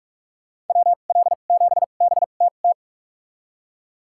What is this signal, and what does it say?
Morse code, heard as a keyed tone, about 46 wpm, spaced as for 33 wpm, 700 Hz. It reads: WL7BTT
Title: USA Call Signs - 6 Characters - 2x Character Spacing - 45wpm